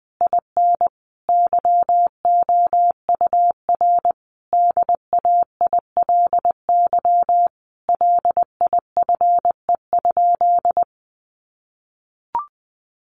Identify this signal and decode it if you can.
Morse code, heard as a keyed tone, about 19 wpm, 700 Hz. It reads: IN YOUR DAILY LIFE?
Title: Sentences from Top 1100 Words 20wpm